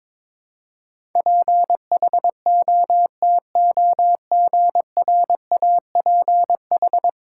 Morse code, heard as a keyed tone, about 21 wpm, 700 Hz. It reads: PHOTOGRAPH
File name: practice / 22wpm / 10 Letter Words - First 3 Letters Given - 22wpm